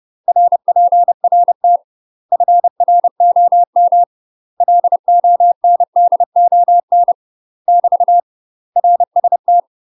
Morse code, heard as a keyed tone, about 30 wpm, 700 Hz. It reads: RPRT FROM LONDON <BT> RST
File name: UK MASTER QSOs TRAINING / 30WPM / UK MASTER QSOs 0012